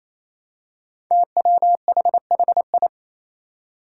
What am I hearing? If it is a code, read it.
Morse code, heard as a keyed tone, about 27 wpm, 700 Hz. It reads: TWHHI
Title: Five Challenging Letter-Number Combinations 28wpm